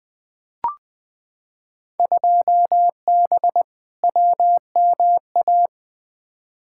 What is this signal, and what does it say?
Morse code, heard as a keyed tone, about 20 wpm, 700 Hz. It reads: 2B WMA
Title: Field Day Exchanges Encore - 20wpm